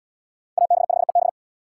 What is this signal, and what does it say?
Morse code, heard as a keyed tone, about 77 wpm, 700 Hz. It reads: AB5EB